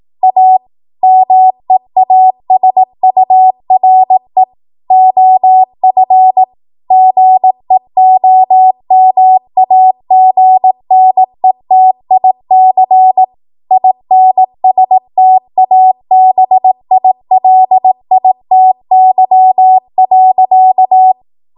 Morse code, heard as a keyed tone, about 18 wpm, 750 Hz. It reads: A MEASURE OF GEOMAGNETIC INSTABILITY.
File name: Archive / 18 WPM / 18WPM